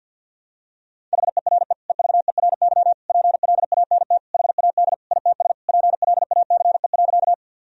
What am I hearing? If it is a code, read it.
Morse code, heard as a keyed tone, about 52 wpm, 700 Hz. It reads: HERE EVERY PLANT HAD ITS PLACE.
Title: Sentences from Top 1000 Words - Review Part 1 - 50wpm